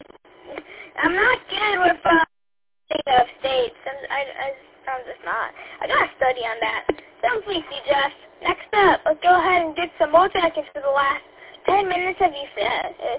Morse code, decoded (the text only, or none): none